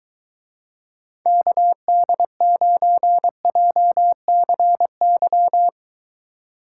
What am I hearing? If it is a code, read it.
Morse code, heard as a keyed tone, about 23 wpm, 700 Hz. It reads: KD9JCY